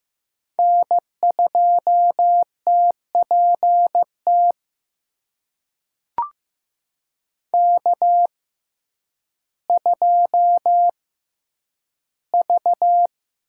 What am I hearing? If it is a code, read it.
Morse code, heard as a keyed tone, about 15 wpm, spaced as for 9 wpm, 700 Hz. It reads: N2TPT K2V
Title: USA Call Signs - 6x Character Spacing - 15wpm